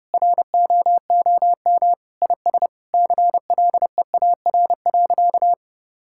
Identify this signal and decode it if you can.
Morse code, heard as a keyed tone, about 32 wpm, 700 Hz. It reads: ROOM IS CLEAR.
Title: Sentences from Top 1000 Words - Review Part 6 - 30wpm